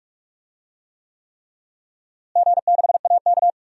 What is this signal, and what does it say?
Morse code, heard as a keyed tone, about 46 wpm, 700 Hz. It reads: G6AK
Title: DXCC-100 Call Signs - 4 Characters - Repeated 3x - 45wpm